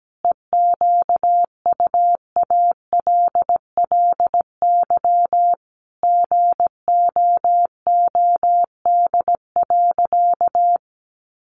Morse code, heard as a keyed tone, about 17 wpm, 700 Hz. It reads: EQUALLY GOOD.
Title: Sentences from Top 1400 Words 17wpm